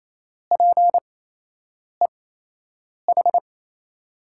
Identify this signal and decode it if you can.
Morse code, heard as a keyed tone, about 28 wpm, spaced as for 8 wpm, 700 Hz. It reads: PEH
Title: USA Call Signs - 5 Characters - 8x Character Spacing - 28wpm